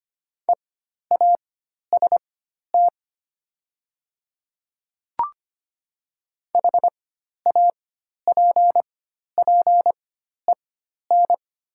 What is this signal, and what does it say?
Morse code, heard as a keyed tone, about 25 wpm, spaced as for 12 wpm, 700 Hz. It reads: EAST HAPPEN
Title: 4x Character Spacing - Top 500 Words - 25wpm